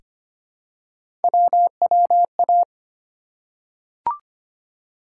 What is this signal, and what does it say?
Morse code, heard as a keyed tone, about 25 wpm, 700 Hz. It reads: WWA
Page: morse-code-ninja-practice-25wpm